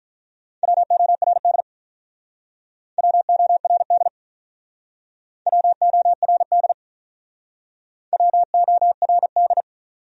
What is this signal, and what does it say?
Morse code, heard as a keyed tone, about 46 wpm, 700 Hz. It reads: WORD WORD WORD WORD